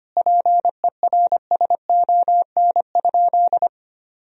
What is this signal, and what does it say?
Morse code, heard as a keyed tone, about 25 wpm, 700 Hz. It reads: PERSON?
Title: Sentences from Top 400 Words and Abbreviations 25wpm